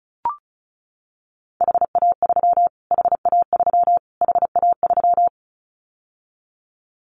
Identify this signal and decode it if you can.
Morse code, heard as a keyed tone, about 34 wpm, 700 Hz. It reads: HA3 HA3 HA3
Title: Callsign Prefixes for Top 25 DXCC - Repeated 3x - 35wpm